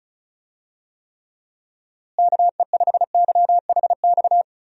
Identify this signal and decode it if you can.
Morse code, heard as a keyed tone, about 36 wpm, 700 Hz. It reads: KE5YHX
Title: USA Call Signs - 6 Characters - Repeated 3x - 35wpm